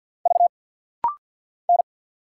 Morse code, heard as a keyed tone, about 50 wpm, 690 Hz. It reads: V N
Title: Single Letter-Number Alt - Rapid-Fire 50wpm